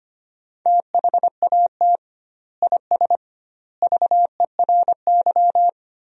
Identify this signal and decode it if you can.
Morse code, heard as a keyed tone, about 25 wpm, 700 Hz. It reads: THAT IS VERY
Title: 2x Word-Spacing - Sentences from Top 400 Words - 25wpm